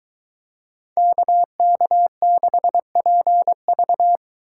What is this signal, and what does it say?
Morse code, heard as a keyed tone, about 23 wpm, 700 Hz. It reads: KK6PV